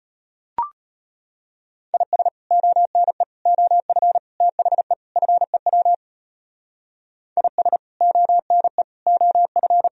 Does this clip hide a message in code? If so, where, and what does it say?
Yes: Morse code from 1.9 to 9.9 s, heard as a keyed tone, about 33 wpm, 700 Hz. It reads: IS ONE OF THE FEW IS ONE OF